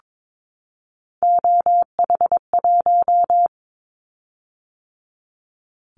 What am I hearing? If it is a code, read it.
Morse code, heard as a keyed tone, about 22 wpm, 700 Hz. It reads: OH1
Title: Callsign Prefixes for Top 50 DXCC - 22wpm